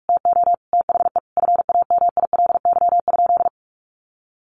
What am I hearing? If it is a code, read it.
Morse code, heard as a keyed tone, about 44 wpm, 700 Hz. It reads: TO THE FAMILY?
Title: Sentences from Top 1400 Words 45wpm